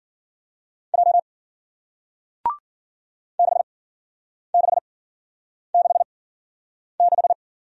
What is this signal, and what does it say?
Morse code, heard as a keyed tone, about 55 wpm, spaced as for 9 wpm, 700 Hz. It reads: J 6666